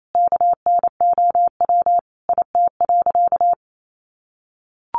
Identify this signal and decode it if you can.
Morse code, heard as a keyed tone, about 28 wpm, 700 Hz. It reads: KNOW IT.
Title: Sentences from Top 900 Words 28wpm